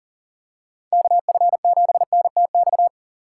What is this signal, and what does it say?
Morse code, heard as a keyed tone, about 40 wpm, 700 Hz. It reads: KF7NTX
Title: USA Call Signs - 3x Character Spacing - 40wpm